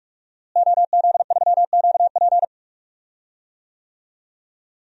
Morse code, heard as a keyed tone, about 42 wpm, 700 Hz. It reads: OZ3QP
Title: DXCC-100 Call Signs - 5 Characters - Repeated 3x - 45wpm